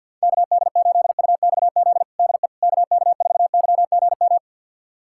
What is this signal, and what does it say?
Morse code, heard as a keyed tone, about 51 wpm, 700 Hz. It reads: KD8UXZ DE KK4YGM
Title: USA Callsign DE Callsign - 6 Characters - 50wpm